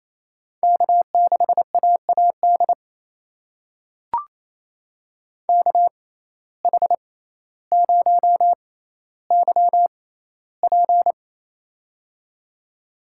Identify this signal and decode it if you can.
Morse code, heard as a keyed tone, about 28 wpm, spaced as for 14 wpm, 700 Hz. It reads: K6AAD KH0YP
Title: USA Call Signs - 5 Characters - 6x Character Spacing - 28wpm